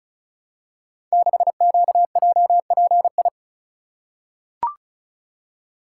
Morse code, heard as a keyed tone, about 36 wpm, 700 Hz. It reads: 6QJPI
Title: Code Groups - 5 Letter:Numbers - 35wpm